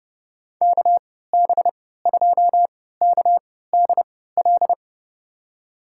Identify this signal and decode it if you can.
Morse code, heard as a keyed tone, about 32 wpm, spaced as for 17 wpm, 700 Hz. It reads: KB2KDL
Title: USA Call Signs - 6 Characters - 3x Character Spacing - 30wpm